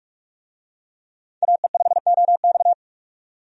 Morse code, heard as a keyed tone, about 46 wpm, 700 Hz. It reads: AE5OX
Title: USA Call Signs - 5 Characters - 45wpm